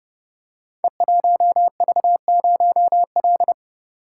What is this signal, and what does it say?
Morse code, heard as a keyed tone, about 28 wpm, 700 Hz. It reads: E1V0L
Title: Code Groups - 5 Letter:Numbers - 30wpm